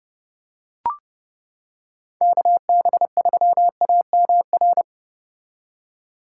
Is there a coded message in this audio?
Yes: Morse code from 2.2 to 4.8 s, heard as a keyed tone, about 30 wpm, 700 Hz. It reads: KB3AMR